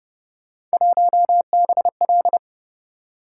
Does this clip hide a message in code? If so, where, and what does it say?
Yes: Morse code from 0.7 to 2.4 s, heard as a keyed tone, about 29 wpm, 700 Hz. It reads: 1BL